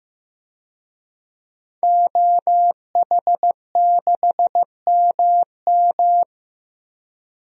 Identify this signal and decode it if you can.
Morse code, heard as a keyed tone, about 16 wpm, 700 Hz. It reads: OH6MM
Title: DXCC-100 Call Signs - 5 Characters - Repeated 3x - 15wpm